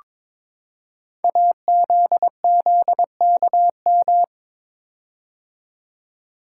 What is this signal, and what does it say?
Morse code, heard as a keyed tone, about 22 wpm, 700 Hz. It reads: AZZKM
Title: Code Groups - 5 Letters - 22wpm